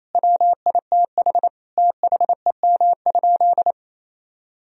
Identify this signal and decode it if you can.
Morse code, heard as a keyed tone, about 28 wpm, 700 Hz. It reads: WITH THEM?
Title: Sentences from Top 1000 Words - Review Part 3 - 28wpm